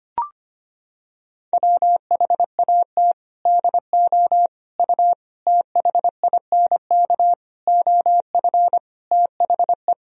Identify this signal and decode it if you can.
Morse code, heard as a keyed tone, about 25 wpm, 700 Hz. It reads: WHAT DO U THINK OF THE
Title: Sentences from Top 300 Words and Abbreviations 25wpm